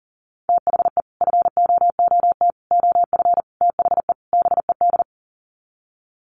Morse code, heard as a keyed tone, about 40 wpm, 700 Hz. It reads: THE FOOT OF THE BED
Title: Sets of 5 Words Encore2 40wpm